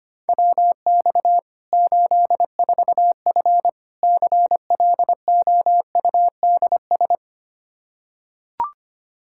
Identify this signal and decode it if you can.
Morse code, heard as a keyed tone, about 25 wpm, 700 Hz. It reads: WX 84F CLOUDS
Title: Practice QSO - Exchange 2 - Info - 25wpm